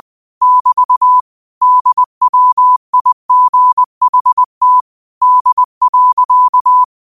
Morse code, heard as a keyed tone, about 20 wpm, 1000 Hz. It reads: <BT> DWIGHT D.